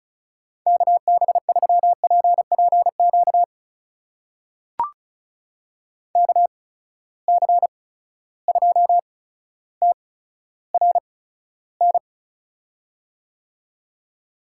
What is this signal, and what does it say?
Morse code, heard as a keyed tone, about 36 wpm, spaced as for 15 wpm, 700 Hz. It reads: KB3PPQ KC2TRN
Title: USA Call Signs - 6 Characters - 8x Character Spacing - 35wpm